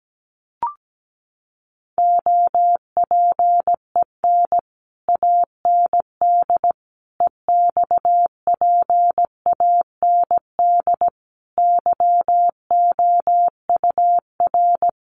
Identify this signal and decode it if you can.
Morse code, heard as a keyed tone, about 17 wpm, 700 Hz. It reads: OPEN AND EXPAND YOUR
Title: Sentences from Top 1400 Words 17wpm